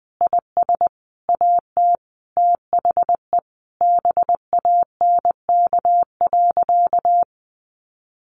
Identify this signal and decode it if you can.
Morse code, heard as a keyed tone, about 20 wpm, 700 Hz. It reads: IS AT THE BANK.
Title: Sentences from Top 1300 Words 20wpm